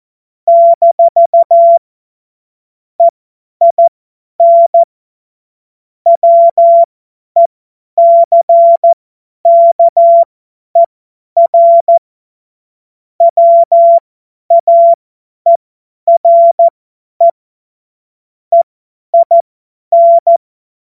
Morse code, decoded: - EIN WECKER WAERE EIN

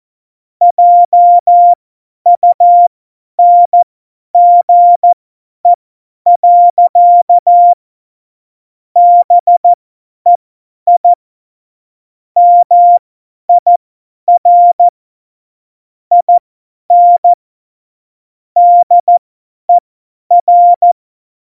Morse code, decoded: JUNGE. BEI MIR IN DER

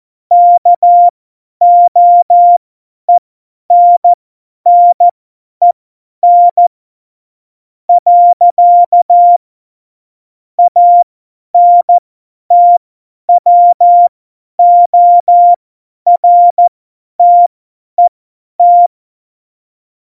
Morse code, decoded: KOENNEN. ANTWORTET

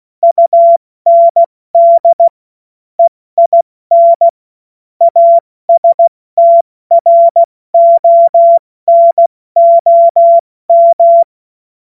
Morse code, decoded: UND EIN ASTRONOM

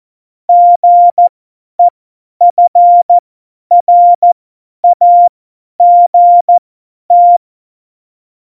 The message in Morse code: GEFRAGT

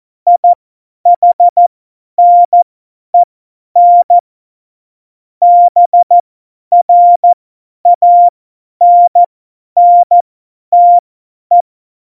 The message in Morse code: IHNEN BRANNTE

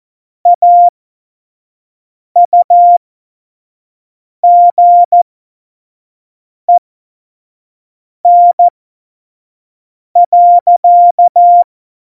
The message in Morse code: AUGEN.

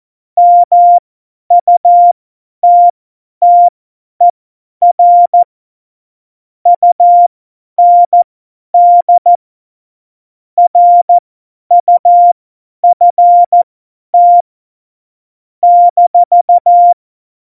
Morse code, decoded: MUTTER UND RUFT -

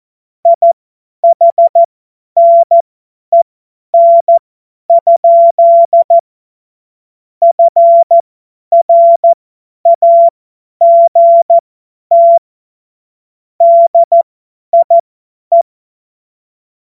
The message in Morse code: IHNEN? FRAGT DIE

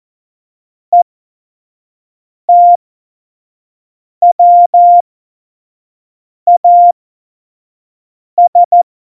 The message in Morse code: ETWAS

